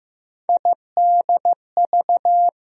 Morse code, heard as a keyed tone, about 15 wpm, 700 Hz. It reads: IDV
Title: USA Call Sign Postfixes - 3 Letters 15wpm